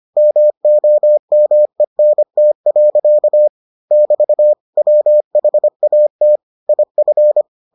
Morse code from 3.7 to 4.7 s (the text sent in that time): <BT>